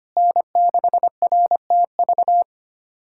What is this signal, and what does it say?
Morse code, heard as a keyed tone, about 25 wpm, 700 Hz. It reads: N6RTV
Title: USA Callsign DE Callsign 25wpm